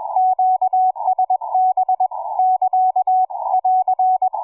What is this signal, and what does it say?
Morse code, heard as a keyed tone, about 21 wpm, 740 Hz, with radio noise. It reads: QSB <KA> <AR>
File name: cw forum / pj4 k8gg 1827 160m 05h26